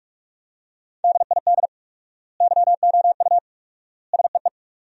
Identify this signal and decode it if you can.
Morse code, heard as a keyed tone, about 42 wpm, 700 Hz. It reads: DID YOU SEE
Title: 4x Word-Spacing - Sentences from Top 100 Words - 45wpm